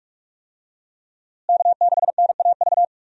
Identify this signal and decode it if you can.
Morse code, heard as a keyed tone, about 44 wpm, 700 Hz. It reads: K6NAV
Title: USA Call Signs - 5 Characters - Repeated 3x - 45wpm